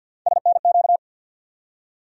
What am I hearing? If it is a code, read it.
Morse code, heard as a keyed tone, about 52 wpm, 700 Hz. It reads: SNQ
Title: USA Call Sign Postfixes - 3 Letters 50wpm